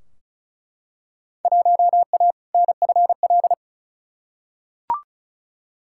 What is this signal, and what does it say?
Morse code, heard as a keyed tone, about 36 wpm, 700 Hz. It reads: 1A NFL
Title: Field Day Exchanges - 35wpm